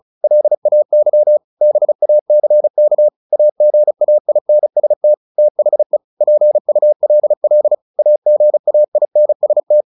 Morse code, heard as a keyed tone, about 36 wpm, 600 Hz. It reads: LAY BACK AGAINST THE PULL AGAINST
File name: the old man and the sea / the old man and the sea 1 35wpm